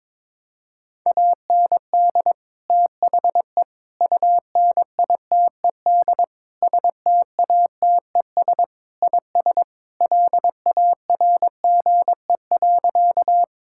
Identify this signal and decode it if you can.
Morse code, heard as a keyed tone, about 22 wpm, 700 Hz. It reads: AND THE UNITED STATES IS LARGE.
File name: practice / 22wpm / Sentences with 4-Word Phrases - 22wpm